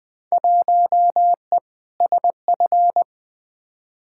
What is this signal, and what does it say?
Morse code, heard as a keyed tone, about 21 wpm, 700 Hz. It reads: 1E SF
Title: Field Day Exchanges - 20wpm